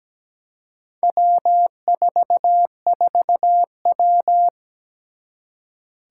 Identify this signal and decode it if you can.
Morse code, heard as a keyed tone, about 17 wpm, 700 Hz. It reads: W44W
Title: Four Challenging Letter-Number Combinations 17wpm